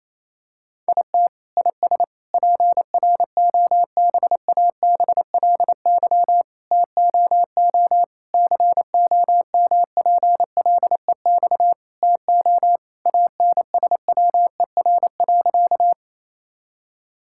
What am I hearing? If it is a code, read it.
Morse code, heard as a keyed tone, about 28 wpm, 700 Hz. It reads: IT IS PROBABLY TOO COMPLEX TO ANSWER.
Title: Sentences from Top 1000 Words - Review Part 6 - 28wpm